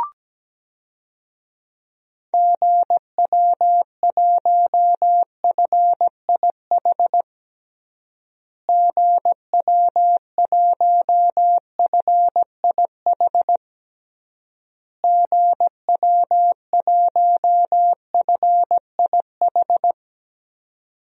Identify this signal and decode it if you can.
Morse code, heard as a keyed tone, about 17 wpm, 700 Hz. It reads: GW1FIH GW1FIH GW1FIH
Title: DXCC-100 Call Signs - 6 Characters - Repeated 3x - 17wpm